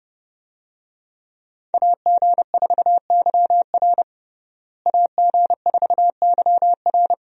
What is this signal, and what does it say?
Morse code, heard as a keyed tone, about 30 wpm, 700 Hz. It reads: AG4YR AG4YR
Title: USA Call Signs - 5 Characters - Repeated 2x - 30wpm